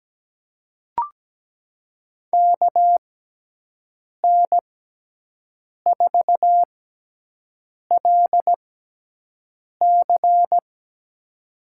Morse code, heard as a keyed tone, about 17 wpm, spaced as for 6 wpm, 700 Hz. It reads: KN4LC
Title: USA Call Signs - 6 Characters - 6x Character Spacing - 17wpm